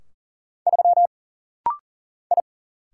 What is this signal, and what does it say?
Morse code, heard as a keyed tone, about 40 wpm, 700 Hz. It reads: 3 I
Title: Single Letter-Number - Rapid-Fire 40wpm